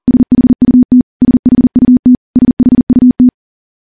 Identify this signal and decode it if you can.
Morse code, heard as a keyed tone, about 40 wpm, 250 Hz. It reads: SHUT SHUT SHUT
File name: morse / morse-training-40WPM